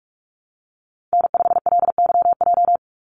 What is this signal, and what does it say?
Morse code, heard as a keyed tone, about 44 wpm, 700 Hz. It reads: N5LYJ
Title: Call Signs Repeated 2x - 45wpm